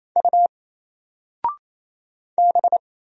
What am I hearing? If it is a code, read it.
Morse code, heard as a keyed tone, about 27 wpm, 700 Hz. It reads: U B
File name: Single-Letter-Number-28wpm